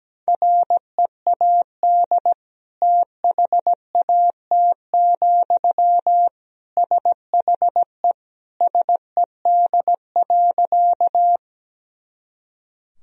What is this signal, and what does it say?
Morse code, heard as a keyed tone, about 17 wpm, 700 Hz. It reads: READ THAT, SHE SED.
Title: Sentences from Top 200 Words and Abbreviations 17wpm